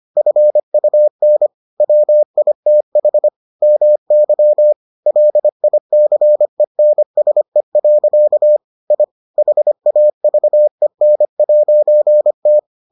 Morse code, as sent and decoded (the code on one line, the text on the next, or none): ..-. ..- -. / .-- .. - .... / -- -.-- / .-.. .. -.-. -. ... .-.-.- / .. / .... .- ...- . -. .----. -
FUN WITH MY LICENSE. I HAVEN'T